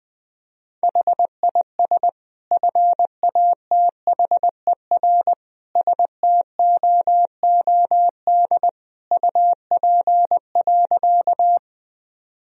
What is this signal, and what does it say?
Morse code, heard as a keyed tone, about 20 wpm, 700 Hz. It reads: HIS FATHER STOOD UP.
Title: Sentences from Top 1000 Words - Review Part 2 - 20wpm